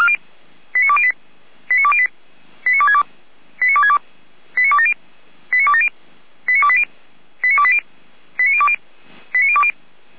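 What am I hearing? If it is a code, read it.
Five-tone selective call (ZVEI), heard as quick runs of five tones, tones of about 70 ms. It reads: 89208 89208 89262 89262 89270 89270 89280 89280 89020 89020
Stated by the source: ZVEI five-tone selective call 89358 received on SDR.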